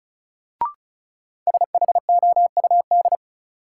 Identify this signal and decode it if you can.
Morse code, heard as a keyed tone, about 34 wpm, 700 Hz. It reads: SHOUD
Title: Sentences from Top 1000 Words - Review Part 4 - New Word Drill - 35wpm